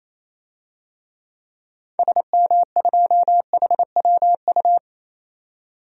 Morse code, heard as a keyed tone, about 27 wpm, 700 Hz. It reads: SM2HWU
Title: DXCC-100 Call Signs - 6 Characters - 28wpm